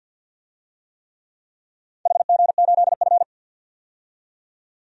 Morse code, heard as a keyed tone, about 53 wpm, 700 Hz. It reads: HG8F
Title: DXCC-100 Call Signs - 4 Characters - 50wpm